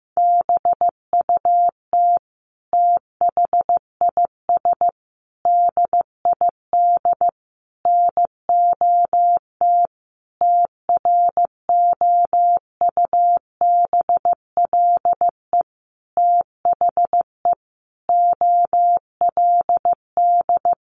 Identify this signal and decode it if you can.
Morse code, heard as a keyed tone, about 15 wpm, 700 Hz. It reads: BUT THIS DID NOT TROUBLE THE OLD
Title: Sentences from Top 1200 Words 15wpm